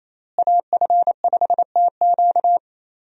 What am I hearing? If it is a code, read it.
Morse code, heard as a keyed tone, about 28 wpm, 700 Hz. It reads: AF5TQ
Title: USA Call Signs - 5 Characters - Repeated 2x - 28wpm